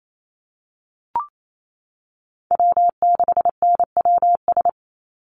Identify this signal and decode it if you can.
Morse code, heard as a keyed tone, about 28 wpm, 700 Hz. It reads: W6NWS
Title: Call Signs Repeated 3x - 28wpm